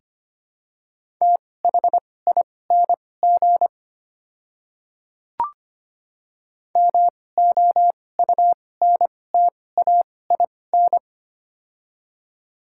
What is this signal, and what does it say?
Morse code, heard as a keyed tone, about 25 wpm, spaced as for 18 wpm, 700 Hz. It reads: THING MOUNTAIN
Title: 2x Character Spacing - Top 300 Words - 25wpm